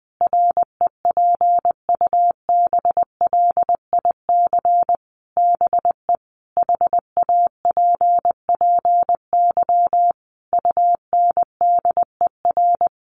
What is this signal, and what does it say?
Morse code, heard as a keyed tone, about 20 wpm, 700 Hz. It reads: REPUBLIC BE HAPPY UNDER
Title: Sentences from Top 1500 Words 20wpm